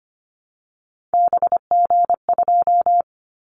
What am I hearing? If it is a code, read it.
Morse code, heard as a keyed tone, about 25 wpm, 700 Hz. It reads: BG2
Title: Callsign Prefixes for Top 100 DXCC - 25wpm